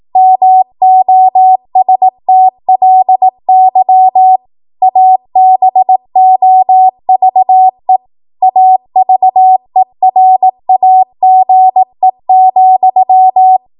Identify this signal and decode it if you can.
Morse code, heard as a keyed tone, about 18 wpm, 750 Hz. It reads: MOSTLY ABOVE AVERAGE,
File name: Morse / 18WPM